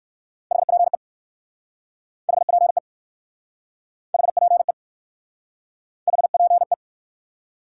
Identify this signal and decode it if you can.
Morse code, heard as a keyed tone, about 59 wpm, 700 Hz. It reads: HPE HPE HPE HPE